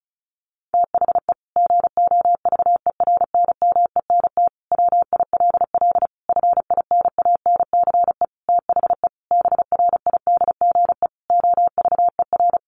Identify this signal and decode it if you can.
Morse code, heard as a keyed tone, about 34 wpm, 700 Hz. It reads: THE GOVERNMENT WILL FINANCE THE BRIDGE OVER
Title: Sentences from Top 1500 Words 35wpm